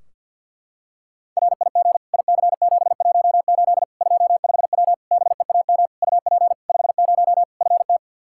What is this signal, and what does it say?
Morse code, heard as a keyed tone, about 46 wpm, 700 Hz. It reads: RIG IC718 25W BEAM UP 50 FT
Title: K1USN SST QSO Practice 50wpm